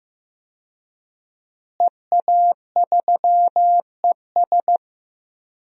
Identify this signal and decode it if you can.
Morse code, heard as a keyed tone, about 15 wpm, 700 Hz. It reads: EA3ES
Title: DXCC-100 Call Signs - 5 Characters - Repeated 2x - 15wpm